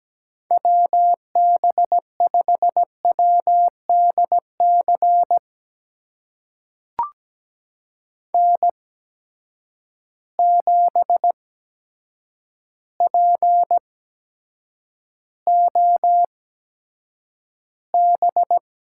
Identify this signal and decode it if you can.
Morse code, heard as a keyed tone, about 17 wpm, spaced as for 8 wpm, 700 Hz. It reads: WB5WDC N7POB
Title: USA Call Signs - 8x Character Spacing - 17wpm